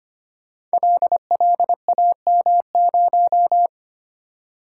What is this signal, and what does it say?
Morse code, heard as a keyed tone, about 25 wpm, 700 Hz. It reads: LLAM0
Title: Code Groups - 5 Letter:Numbers - 25wpm